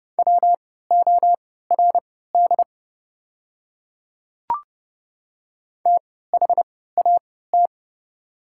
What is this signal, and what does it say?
Morse code, heard as a keyed tone, about 30 wpm, spaced as for 17 wpm, 700 Hz. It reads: WORD THAT
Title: 3x Character Spacing - Top 100 Words - 30wpm